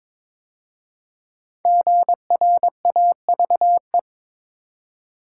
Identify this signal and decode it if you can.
Morse code, heard as a keyed tone, about 22 wpm, 700 Hz. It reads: GRAVE
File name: 5 Letter Words - First 3 Letters Given - 22wpm